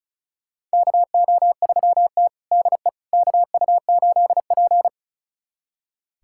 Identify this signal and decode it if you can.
Morse code, heard as a keyed tone, about 36 wpm, 700 Hz. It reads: KO3T DE KU8P
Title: USA Callsign DE Callsign - 4 Characters - 35wpm